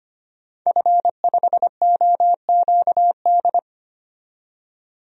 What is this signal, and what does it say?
Morse code, heard as a keyed tone, about 25 wpm, 700 Hz. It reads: F5OQD
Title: DXCC-100 Call Signs - 5 Characters - 25wpm